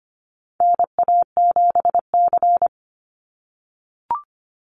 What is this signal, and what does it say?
Morse code, heard as a keyed tone, about 25 wpm, 700 Hz. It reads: NA7C
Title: Call Signs - 25wpm